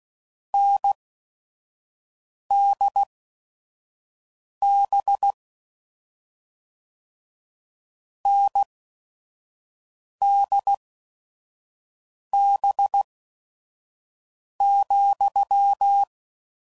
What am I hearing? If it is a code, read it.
Morse code, heard as a keyed tone, about 16 wpm, spaced as for 5 wpm, 800 Hz. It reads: NDB NDB,
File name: Combinations / Difficult Combos / NDB NDB6